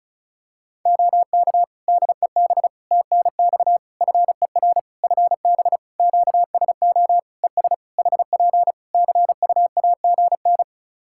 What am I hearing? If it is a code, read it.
Morse code, heard as a keyed tone, about 36 wpm, 700 Hz. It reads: OK DEB TNX FER FB QSO ES HP CUAGN
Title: Practice QSO - Exchange 4 - Ending - 35wpm